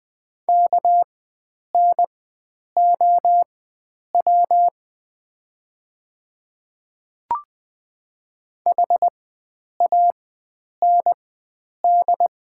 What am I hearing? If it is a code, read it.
Morse code, heard as a keyed tone, about 20 wpm, spaced as for 9 wpm, 700 Hz. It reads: KNOW HAND
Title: 4x Character Spacing - Top 300 Words - 20wpm